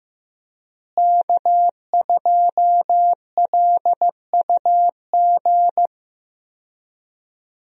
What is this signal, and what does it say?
Morse code, heard as a keyed tone, about 15 wpm, 700 Hz. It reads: K2LUG
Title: USA Call Signs - 5 Characters - Repeated 3x - 15wpm